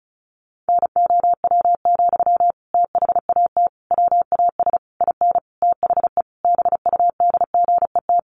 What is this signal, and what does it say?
Morse code, heard as a keyed tone, about 34 wpm, 700 Hz. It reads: NOW, THAT WAS IN THE BUDGET
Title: Sentences from Top 1200 Words 35wpm